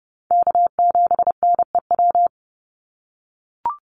Morse code, heard as a keyed tone, about 30 wpm, 700 Hz. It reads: K7NEW
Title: NA SOTA Chaser Callsigns 30wpm